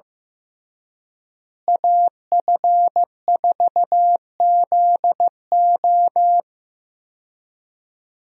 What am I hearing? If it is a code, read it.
Morse code, heard as a keyed tone, about 15 wpm, 700 Hz. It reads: AF4ZO
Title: USA Call Signs - 5 Characters - Repeated 3x - 15wpm